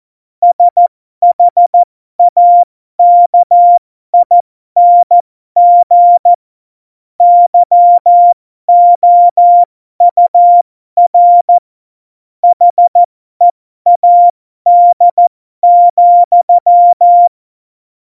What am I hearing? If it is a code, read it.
Morse code, heard as a keyed tone, about 14 wpm, spaced as for 12 wpm, 700 Hz. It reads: SHAKING YOUR HEAD,
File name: Morsekurs / Witze